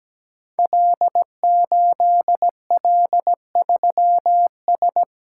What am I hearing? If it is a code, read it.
Morse code, heard as a keyed tone, about 17 wpm, 700 Hz. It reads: L8L3S
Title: Code Groups - 5 Letter:Numbers - 17wpm